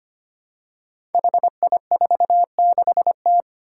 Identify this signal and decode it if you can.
Morse code, heard as a keyed tone, about 25 wpm, 700 Hz. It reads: HI46T
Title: Five Challenging Letter-Number Combinations 25wpm